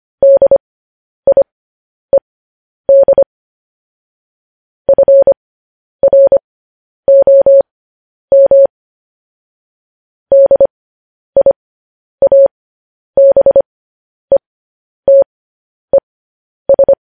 Morse code, heard as a keyed tone, about 25 wpm, spaced as for 10 wpm, 550 Hz. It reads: DIED FROM DIABETES
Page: News Headlines in Morse Code at 10 WPM